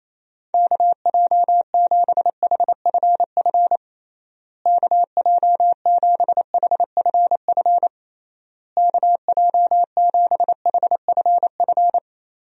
Morse code, heard as a keyed tone, about 28 wpm, 700 Hz. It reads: KJ7HFF KJ7HFF KJ7HFF
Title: USA Call Signs - 6 Characters - Repeated 3x - 28wpm